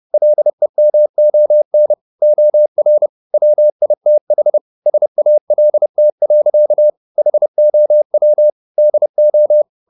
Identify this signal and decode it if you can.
Morse code, heard as a keyed tone, about 30 wpm, 600 Hz. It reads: LEMON OR WITH SALT. HOW DO
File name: the old man and the sea / the old man and the sea 1 30wpm